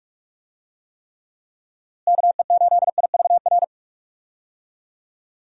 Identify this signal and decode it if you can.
Morse code, heard as a keyed tone, about 46 wpm, 700 Hz. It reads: KE8IVR
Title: USA Call Signs - 6 Characters - Repeated 2x - 45wpm